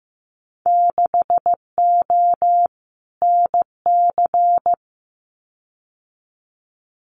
Morse code, heard as a keyed tone, about 15 wpm, 700 Hz. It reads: 6O NC
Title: Winter Field Day Exchanges 15wpm